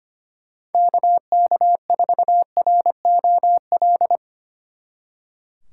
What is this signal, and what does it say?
Morse code, heard as a keyed tone, about 31 wpm, 700 Hz. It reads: KK4ROL
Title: Speed Racing - USA 2x3 Call Signs 25wpm